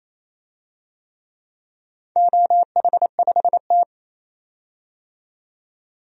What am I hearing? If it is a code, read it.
Morse code, heard as a keyed tone, about 28 wpm, 700 Hz. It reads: OH5T
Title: DXCC-100 Call Signs - 4 Characters - 28wpm